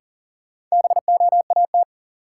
Morse code, heard as a keyed tone, about 40 wpm, 700 Hz. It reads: BOAT
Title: Sentences from Top 500 Words - New Word Drill - 40wpm